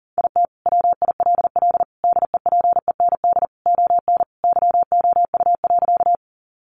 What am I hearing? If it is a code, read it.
Morse code, heard as a keyed tone, about 40 wpm, 700 Hz. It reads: IT WILL DEPEND ON YOU.
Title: Sentences from Top 1300 Words 40wpm